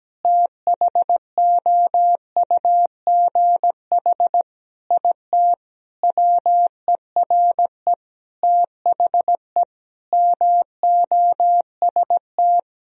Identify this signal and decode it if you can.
Morse code, heard as a keyed tone, about 17 wpm, 700 Hz. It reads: THOUGH IT WERE THE MOST
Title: Sentences from Top 1000 Words - Review Part 3 - 17wpm